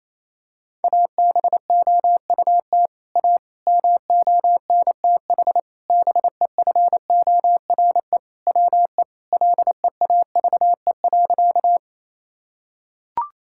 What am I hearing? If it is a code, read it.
Morse code, heard as a keyed tone, about 28 wpm, 700 Hz. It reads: ABOUT A MONTH BEFORE WE LEAVE.
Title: Sentences from Top 1000 Words - Review Part 5 - 28wpm